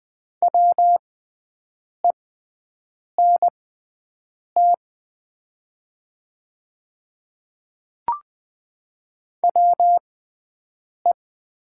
Morse code, heard as a keyed tone, about 20 wpm, spaced as for 7 wpm, 700 Hz. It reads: WENT WE
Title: 6x Character Spacing - Top 500 Words - 20wpm